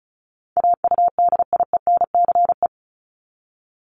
Morse code, heard as a keyed tone, about 34 wpm, 700 Hz. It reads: AUDIENCE
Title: Top 751-1000 Words 35wpm